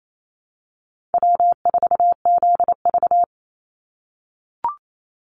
Morse code, heard as a keyed tone, about 28 wpm, 700 Hz. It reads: W4ZV
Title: NA SOTA Chaser Callsigns 28wpm